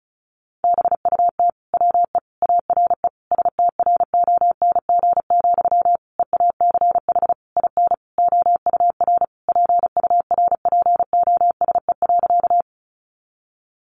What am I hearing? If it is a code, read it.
Morse code, heard as a keyed tone, about 34 wpm, 700 Hz. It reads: BUT WE ARE STRONG, EACH IN OUR PURPOSE.
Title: Sentences from Top 800 Words 35wpm